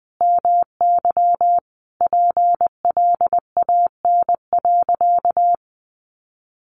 Morse code, heard as a keyed tone, about 20 wpm, 700 Hz. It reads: MY PLAN.
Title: Sentences from Top 500 Words 20wpm